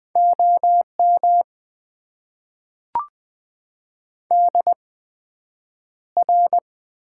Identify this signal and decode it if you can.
Morse code, heard as a keyed tone, about 20 wpm, spaced as for 9 wpm, 700 Hz. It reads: OM DR
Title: Top 90 QSO Elements - 8x Character Spacing - 20wpm